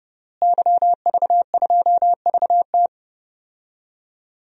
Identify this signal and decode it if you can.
Morse code, heard as a keyed tone, about 32 wpm, 700 Hz. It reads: YV2VT
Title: DXCC-100 Call Signs - 5 Characters - Repeated 3x - 30wpm